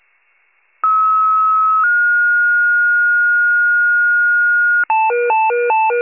Two-tone page, heard as a one-second tone, then a three-second tone, about 1290 then 1430 Hz. It is heard from 0.8 to 4.8 s.
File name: pager / 84 FIRE / wcalert